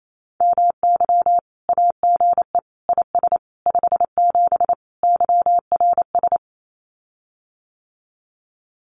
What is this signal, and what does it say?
Morse code, heard as a keyed tone, about 28 wpm, 700 Hz. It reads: MY AGE IS 57 YRS
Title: Common QSO Phrases Encore 28wpm